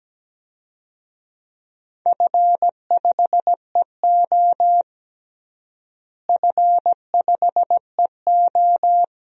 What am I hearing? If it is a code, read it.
Morse code, heard as a keyed tone, about 17 wpm, 700 Hz. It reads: F5EO F5EO
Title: DXCC-100 Call Signs - 4 Characters - Repeated 2x - 17wpm